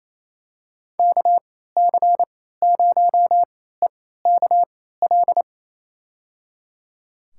Morse code, heard as a keyed tone, about 27 wpm, spaced as for 16 wpm, 700 Hz. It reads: KC0EKL
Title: USA Call Signs - 6 Characters - 3x Character Spacing - 28wpm